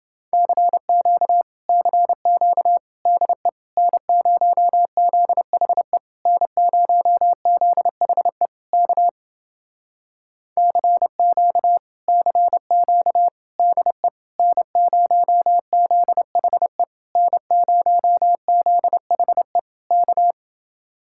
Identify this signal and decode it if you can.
Morse code, heard as a keyed tone, about 29 wpm, 700 Hz. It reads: CQ CQ DE N0ZHE N0ZHE K CQ CQ DE N0ZHE N0ZHE K